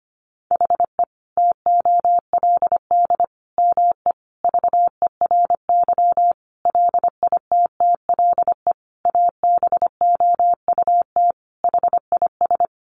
Morse code, heard as a keyed tone, about 25 wpm, 700 Hz. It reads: HE TOLD ME VERY LITTLE ABOUT HIS
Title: Sentences from Top 1500 Words 25wpm